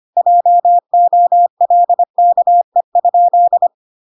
Morse code, heard as a keyed tone, about 25 wpm, 700 Hz. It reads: JOLKE?